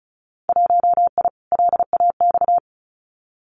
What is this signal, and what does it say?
Morse code, heard as a keyed tone, about 36 wpm, 700 Hz. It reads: 1I LAX
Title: Winter Field Day Exchanges 35wpm